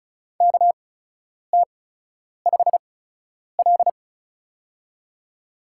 Morse code, heard as a keyed tone, about 34 wpm, spaced as for 10 wpm, 700 Hz. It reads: KT5L
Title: USA Call Signs - 4 Characters - 8x Character Spacing - 35wpm